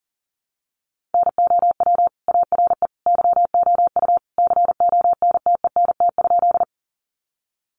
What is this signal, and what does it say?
Morse code, heard as a keyed tone, about 40 wpm, 700 Hz. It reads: NOW ARE YOU CONTENT?
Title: Sentences from Top 900 Words 40wpm